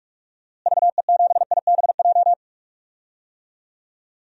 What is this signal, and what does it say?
Morse code, heard as a keyed tone, about 46 wpm, 700 Hz. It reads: VE7IBJ
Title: DXCC-100 Call Signs - 6 Characters - Repeated 3x - 45wpm